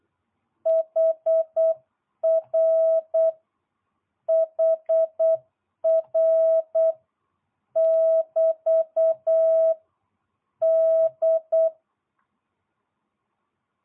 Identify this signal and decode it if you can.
Morse code, heard as a keyed tone, about 8 wpm, 650 Hz. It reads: HR HR <BT> D